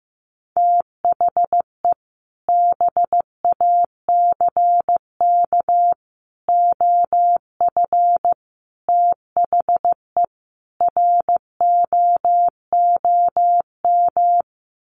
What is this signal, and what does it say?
Morse code, heard as a keyed tone, about 15 wpm, 700 Hz. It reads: THE BACK OF THE ROOM
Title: Sets of 5 Words Encore 15wpm